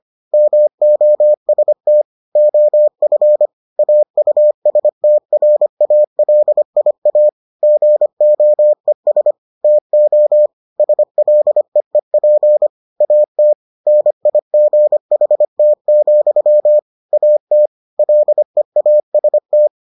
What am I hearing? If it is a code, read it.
Morse code, heard as a keyed tone, about 25 wpm, 600 Hz. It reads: MOST OF AUSTRALIA GOES TO SLEEP AT NIGHT, AT LEAST